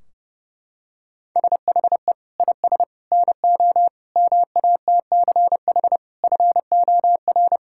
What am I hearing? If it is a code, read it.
Morse code, heard as a keyed tone, about 32 wpm, 700 Hz. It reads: SHE IS NO MATCH FOR
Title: Sentences from Top 1600 Words 30wpm